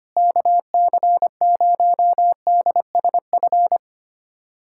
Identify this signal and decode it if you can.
Morse code, heard as a keyed tone, about 25 wpm, 700 Hz. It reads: KC0DSF
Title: USA Callsign DE Callsign 25wpm